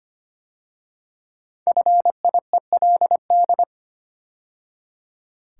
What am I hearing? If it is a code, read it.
Morse code, heard as a keyed tone, about 26 wpm, 700 Hz. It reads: FIELD
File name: 5 Letter Words - First 3 Letters Given - 25wpm